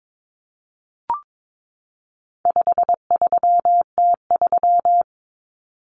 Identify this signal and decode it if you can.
Morse code, heard as a keyed tone, about 22 wpm, 700 Hz. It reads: 53T3
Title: Common Cut Numbers - 4-Digit 22wpm